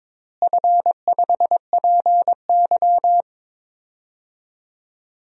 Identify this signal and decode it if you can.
Morse code, heard as a keyed tone, about 22 wpm, 700 Hz. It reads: F5PY
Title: DXCC-100 Call Signs - 4 Characters - Repeated 2x - 22wpm